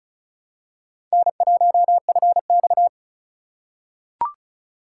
Morse code, heard as a keyed tone, about 35 wpm, 700 Hz. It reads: N1FX
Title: USA Call Signs - 4 Characters - 3x Character Spacing - 35wpm